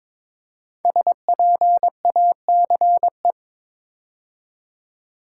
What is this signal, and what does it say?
Morse code, heard as a keyed tone, about 22 wpm, 690 Hz. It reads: SPACE
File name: 5 Letter Words - First 3 Letters Given - 22wpm